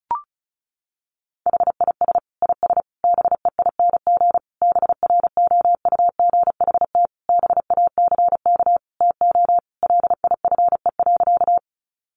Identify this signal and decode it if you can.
Morse code, heard as a keyed tone, about 34 wpm, 700 Hz. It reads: HIS IS BEING BROUGHT BACK TO LIFE.
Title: Sentences from Top 600 Words 35wpm